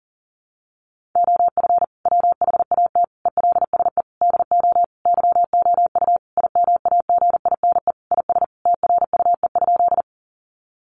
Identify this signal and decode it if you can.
Morse code, heard as a keyed tone, about 40 wpm, 700 Hz. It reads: OF WHAT ELSE DO YOU IMAGINE IS TRUE?
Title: Sentences from Top 1400 Words 40wpm